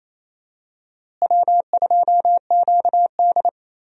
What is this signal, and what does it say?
Morse code, heard as a keyed tone, about 28 wpm, 700 Hz. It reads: W2QD
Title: USA Call Signs - 4 Characters - Repeated 3x - 28wpm